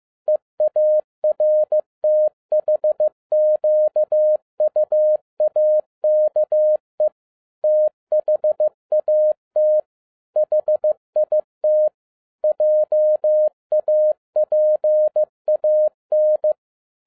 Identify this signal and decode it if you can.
Morse code, heard as a keyed tone, about 15 wpm, 610 Hz. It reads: EARTHQUAKE THAT HIT JAPAN
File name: 2011-10-mp3 / n02-FIELD DAY CONTEST QRV FROM SAKAE VILLAGE 15wpm